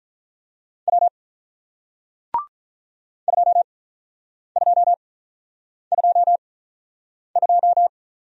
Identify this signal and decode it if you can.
Morse code, heard as a keyed tone, about 47 wpm, 700 Hz. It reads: W 2 2 2 2